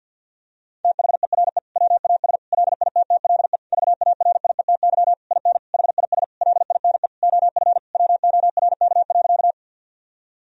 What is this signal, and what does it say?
Morse code, heard as a keyed tone, about 50 wpm, 700 Hz. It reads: THERE WAS LITTLE VARIETY IN HIS LINE OF WORK.
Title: Sentences from Top 1000 Words - Review Part 4 - 50wpm